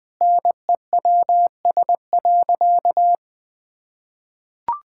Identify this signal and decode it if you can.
Morse code, heard as a keyed tone, about 19 wpm, 700 Hz. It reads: NEWS.
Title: Sentences with 2-Word Phrases - 20wpm